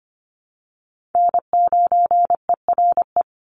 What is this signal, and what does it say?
Morse code, heard as a keyed tone, about 25 wpm, 700 Hz. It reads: N9ERE
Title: USA Call Signs 25wpm